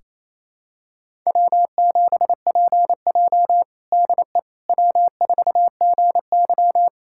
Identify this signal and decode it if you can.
Morse code, heard as a keyed tone, about 28 wpm, 700 Hz. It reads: W7PJ DE W4GY
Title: USA Callsign DE Callsign - 4 Characters - 28wpm